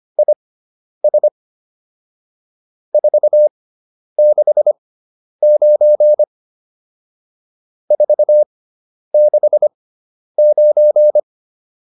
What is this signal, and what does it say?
Morse code, heard as a keyed tone, about 25 wpm, spaced as for 10 wpm, 600 Hz. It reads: IS 469 469
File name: QSO309 10